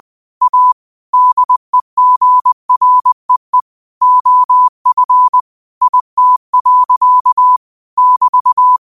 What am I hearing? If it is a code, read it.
Morse code, heard as a keyed tone, about 20 wpm, 1000 Hz. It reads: A DEGREE OF IT. <BT>